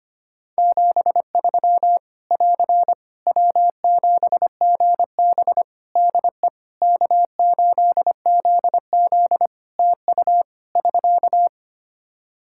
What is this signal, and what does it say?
Morse code, heard as a keyed tone, about 25 wpm, 700 Hz. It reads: 73 <AR> W7GB DE K8ZZ TU <SK>
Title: Practice QSO - Exchange 4 - Ending - 25wpm